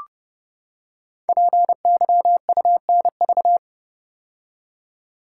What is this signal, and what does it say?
Morse code, heard as a keyed tone, about 30 wpm, 700 Hz. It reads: PYUNV
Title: Code Groups - 5 Letters - 30wpm